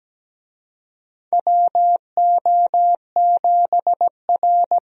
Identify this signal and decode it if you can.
Morse code, heard as a keyed tone, about 17 wpm, 700 Hz. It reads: WO7R
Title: USA Call Signs - 4 Characters - Repeated 3x - 17wpm